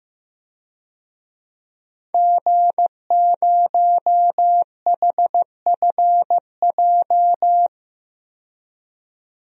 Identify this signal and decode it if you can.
Morse code, heard as a keyed tone, about 15 wpm, 700 Hz. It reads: G0HFJ
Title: DXCC-100 Call Signs - 5 Characters - 15wpm